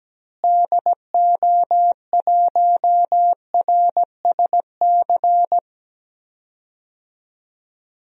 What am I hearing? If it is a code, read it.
Morse code, heard as a keyed tone, about 17 wpm, 700 Hz. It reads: DO1RSC
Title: DXCC-100 Call Signs - 6 Characters - Repeated 2x - 17wpm